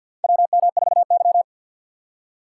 Morse code, heard as a keyed tone, about 49 wpm, 700 Hz. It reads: WM4Y
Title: USA Call Signs - 4 Characters - 3x Character Spacing - 50wpm